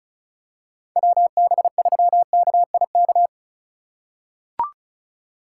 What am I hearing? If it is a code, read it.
Morse code, heard as a keyed tone, about 36 wpm, 700 Hz. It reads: WB3KIK
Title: USA Call Signs - 2x Character Spacing - 35wpm